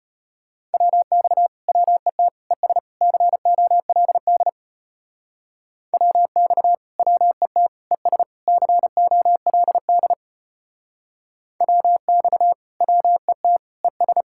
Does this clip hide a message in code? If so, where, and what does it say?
Yes: Morse code from 0.7 to 14.2 s, heard as a keyed tone, about 33 wpm, 700 Hz. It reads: WX WET ES COLD WX WET ES COLD WX WET ES